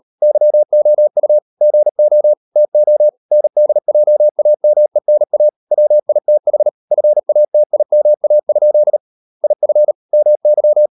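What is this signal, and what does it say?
Morse code, heard as a keyed tone, about 38 wpm, 600 Hz. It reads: YOU GO TO NDJAMENA WITH FATIMA? IF MY
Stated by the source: Morse code practice at 38 WPM, sending Ken Follett's novel Never in English, from CW med Gnister.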